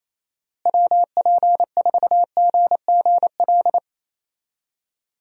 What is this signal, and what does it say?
Morse code, heard as a keyed tone, about 27 wpm, 700 Hz. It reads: WP4GGL
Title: USA Call Signs - 6 Characters - Repeated 3x - 28wpm